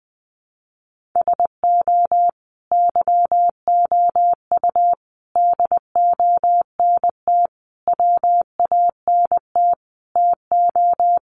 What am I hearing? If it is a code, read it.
Morse code, heard as a keyed tone, about 20 wpm, 700 Hz. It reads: SO YOU DONT WANT TO
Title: Sentences from Top 1000 Words 20wpm